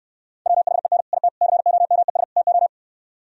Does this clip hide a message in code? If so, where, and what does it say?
Yes: Morse code from 0.5 to 2.7 s, heard as a keyed tone, about 67 wpm, 700 Hz. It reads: WHEN IT COMES TO